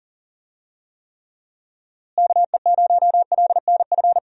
Morse code, heard as a keyed tone, about 40 wpm, 700 Hz. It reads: KE0LNF
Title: USA Call Signs - 6 Characters - Repeated 3x - 40wpm